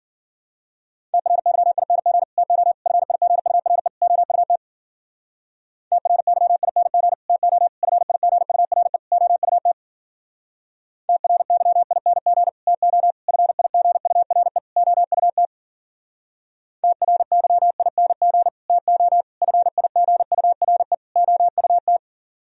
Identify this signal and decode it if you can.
Morse code, heard as a keyed tone, about 52 wpm, 700 Hz. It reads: TRYING TO FIGURE OUT TRYING TO FIGURE OUT TRYING TO FIGURE OUT TRYING TO FIGURE OUT